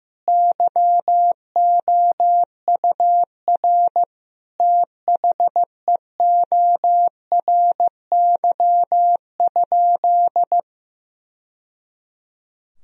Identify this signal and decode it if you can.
Morse code, heard as a keyed tone, about 15 wpm, 700 Hz. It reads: YOUR THEORY?
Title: Sentences from Top 1000 Words - Review Part 3 - 15wpm